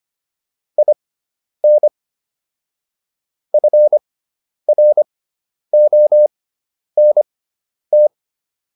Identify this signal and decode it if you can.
Morse code, heard as a keyed tone, about 25 wpm, spaced as for 10 wpm, 600 Hz. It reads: IN FRONT